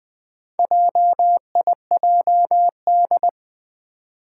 Morse code, heard as a keyed tone, about 20 wpm, 700 Hz. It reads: JIJD
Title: Four Challenging Letter-Number Combinations 20wpm